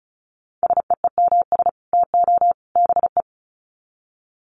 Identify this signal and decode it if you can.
Morse code, heard as a keyed tone, about 34 wpm, 700 Hz. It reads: SEEMS TO BE
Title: Sets of 3 Words 35wpm